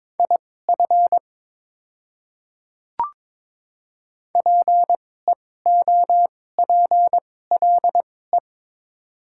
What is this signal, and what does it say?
Morse code, heard as a keyed tone, about 22 wpm, spaced as for 16 wpm, 690 Hz. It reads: IF PEOPLE
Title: 2x Character Spacing - Top 100 Words - 22wpm